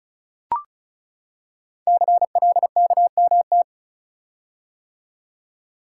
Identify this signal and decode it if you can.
Morse code, heard as a keyed tone, about 36 wpm, 690 Hz. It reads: CLKMT
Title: Code Groups - 5 Letters - 35wpm